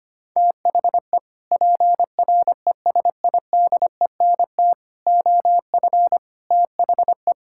Morse code, heard as a keyed tone, about 25 wpm, 700 Hz. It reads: THE PRESIDENT OF THE
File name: practice / 25wpm / Sentences with 3-Word Phrases - 25wpm